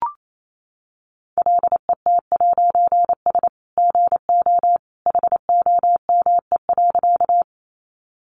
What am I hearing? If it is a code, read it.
Morse code, heard as a keyed tone, about 28 wpm, 700 Hz. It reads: LET'S GO HOME.
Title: A Sound Of Thunder 28wpm